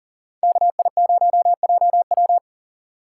Morse code, heard as a keyed tone, about 42 wpm, 700 Hz. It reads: KI0JW
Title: Speed Racing - USA Call Signs 40wpm